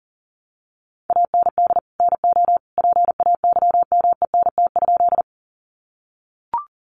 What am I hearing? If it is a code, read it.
Morse code, heard as a keyed tone, about 40 wpm, 700 Hz. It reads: AND NO PAYMENT?
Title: Sentences from Top 1200 Words 40wpm